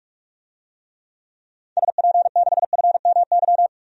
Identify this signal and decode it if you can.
Morse code, heard as a keyed tone, about 46 wpm, 690 Hz. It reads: SP6FMY
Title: DXCC-100 Call Signs - 6 Characters - 45wpm